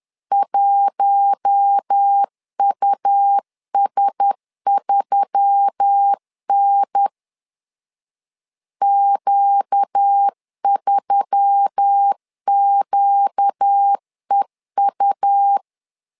Morse code, heard as a keyed tone, about 10 wpm, 800 Hz. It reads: 1US3N Q3QEU
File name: LEZIONI / 5 LBVC / 5 L 10wpm 2259